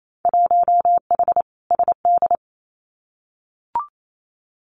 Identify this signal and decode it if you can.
Morse code, heard as a keyed tone, about 28 wpm, 700 Hz. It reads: 1H SD
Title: Winter Field Day Exchanges 28wpm